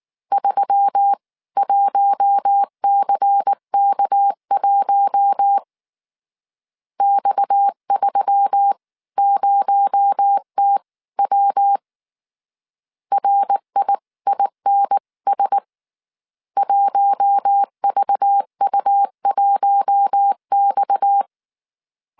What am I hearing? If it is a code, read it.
Morse code, heard as a keyed tone, about 19 wpm, spaced as for 17 wpm, 800 Hz. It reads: 31CK1 X30TW RIINS 1VU1X
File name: CW / DET / 1 6 1500C S C 18wpm 2356